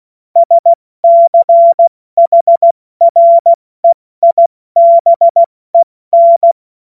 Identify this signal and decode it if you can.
Morse code, heard as a keyed tone, about 16 wpm, 680 Hz. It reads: SCHREIBEN
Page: cw-witze